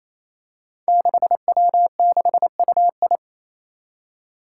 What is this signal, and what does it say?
Morse code, heard as a keyed tone, about 27 wpm, 700 Hz. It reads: 6W6UI
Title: Five Challenging Letter-Number Combinations 28wpm